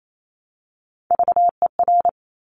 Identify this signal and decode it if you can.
Morse code, heard as a keyed tone, about 28 wpm, 700 Hz. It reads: VER
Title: Top Three Letter Combinations in Words 28wpm